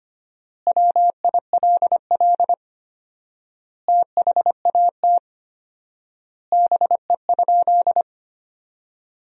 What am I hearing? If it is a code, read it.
Morse code, heard as a keyed tone, about 25 wpm, 700 Hz. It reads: WILL THAT BE?
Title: 4x Word-Spacing - Sentences from Top 100 Words - 25wpm